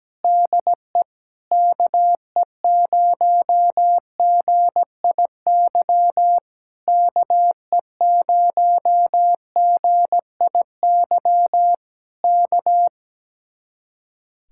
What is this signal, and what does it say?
Morse code, heard as a keyed tone, about 17 wpm, 700 Hz. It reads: DE KE0GIY KE0GIY K
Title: Speed-Racing Common QSO Phrases - 17wpm